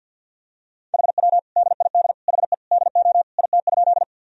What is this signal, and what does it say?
Morse code, heard as a keyed tone, about 53 wpm, 700 Hz. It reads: HW DID HE DO IT?
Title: Sentences from Top 100 Words and Abbreviations 50wpm